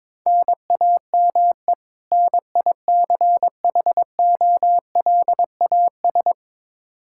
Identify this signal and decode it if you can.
Morse code, heard as a keyed tone, about 22 wpm, 700 Hz. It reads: NAME NICHOLAS
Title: K1USN SST QSO Practice 22wpm